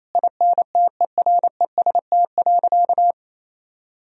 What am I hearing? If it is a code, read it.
Morse code, heard as a keyed tone, about 28 wpm, 700 Hz. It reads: INTEREST.
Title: Sentences from Top 1000 Words - Review Part 4 - 28wpm